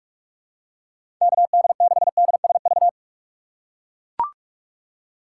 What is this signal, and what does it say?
Morse code, heard as a keyed tone, about 43 wpm, 700 Hz. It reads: KD6DSV
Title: USA Call Signs - 6 Characters - 3x Character Spacing - 45wpm